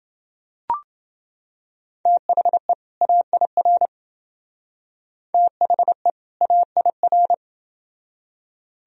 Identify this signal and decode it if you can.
Morse code, heard as a keyed tone, about 28 wpm, 700 Hz. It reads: THE AIR THE AIR